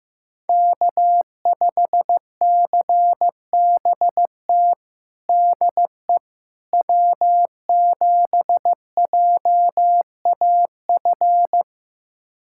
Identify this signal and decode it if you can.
Morse code, heard as a keyed tone, about 15 wpm, 700 Hz. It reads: K5CBT DE W7JAF
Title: USA Callsign DE Callsign - 5 Characters - 15wpm